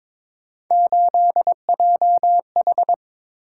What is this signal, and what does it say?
Morse code, heard as a keyed tone, about 22 wpm, 700 Hz. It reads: 8JH
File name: practice / 22wpm / Three Letter-Number Combinations - 22wpm